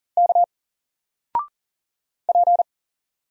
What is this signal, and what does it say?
Morse code, heard as a keyed tone, about 39 wpm, 700 Hz. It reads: K P
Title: Single Character (Letter:Number:Punctuation:Prosign) 40wpm